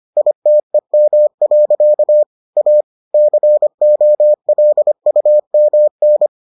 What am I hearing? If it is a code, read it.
Morse code, heard as a keyed tone, about 25 wpm, 600 Hz. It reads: ITEM. A COLUMN